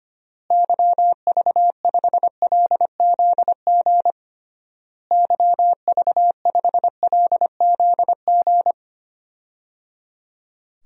Morse code, heard as a keyed tone, about 25 wpm, 700 Hz. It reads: YV5LZG YV5LZG
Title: DXCC-100 Call Signs - 6 Characters - Repeated 3x - 25wpm